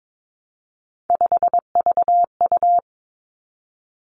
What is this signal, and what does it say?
Morse code, heard as a keyed tone, about 22 wpm, 700 Hz. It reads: 5VU
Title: Three Challenging Letter-Number Combinations 22wpm